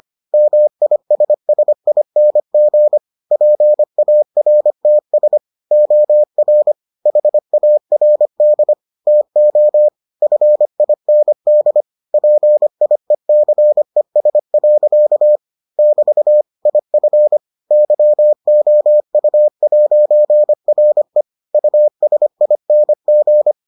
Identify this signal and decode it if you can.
Morse code, heard as a keyed tone, about 25 wpm, 600 Hz. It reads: MISSING PARTS OR HARD TO FIND PIECES. <BT> IF YOU'RE USING